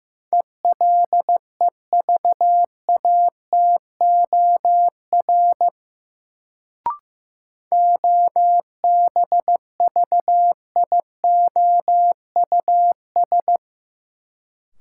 Morse code, heard as a keyed tone, about 15 wpm, 700 Hz. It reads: ELEVATOR OBVIOUS
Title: Sentences from Top 1600 Words - New Word Drill Alt - 15wpm